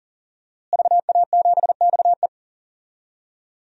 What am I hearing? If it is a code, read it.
Morse code, heard as a keyed tone, about 40 wpm, 700 Hz. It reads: VA7XE
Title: DXCC-100 Call Signs - 5 Characters - Repeated 3x - 40wpm